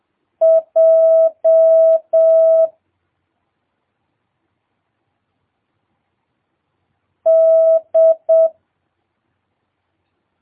Morse code, heard as a keyed tone, about 7 wpm, 650 Hz. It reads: J D
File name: jh8cbh / 2022cw